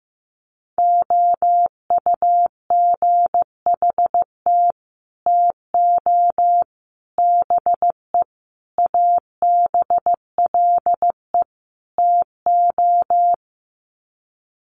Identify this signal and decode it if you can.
Morse code, heard as a keyed tone, about 15 wpm, 700 Hz. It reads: OUGHT TO BE ABLE TO
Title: Sets of 5 Words Encore2 15wpm